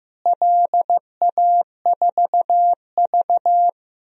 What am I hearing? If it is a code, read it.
Morse code, heard as a keyed tone, about 15 wpm, 700 Hz. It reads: LA4V
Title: DXCC-100 Call Signs - 4 Characters - Repeated 2x - 15wpm